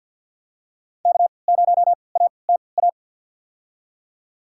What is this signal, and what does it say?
Morse code, heard as a keyed tone, about 47 wpm, spaced as for 28 wpm, 700 Hz. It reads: K0ATA
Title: USA Call Signs - 5 Characters - 3x Character Spacing - 50wpm